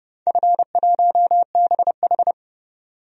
Morse code, heard as a keyed tone, about 32 wpm, 700 Hz. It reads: F1BH
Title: DXCC-100 Call Signs - 4 Characters - 30wpm